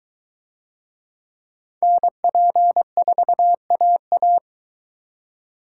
Morse code, heard as a keyed tone, about 22 wpm, 700 Hz. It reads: NP4AA